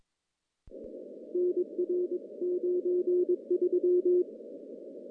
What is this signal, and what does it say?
Morse code, heard as a keyed tone, about 22 wpm, 360 Hz. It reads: NR 93
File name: contests / CW-SS-k5tr- -1500